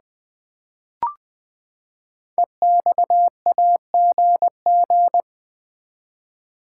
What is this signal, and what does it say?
Morse code, heard as a keyed tone, about 20 wpm, 700 Hz. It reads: EXAGG
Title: Build A Word - 5 Letters At A Time - 10-Letter Words - 20wpm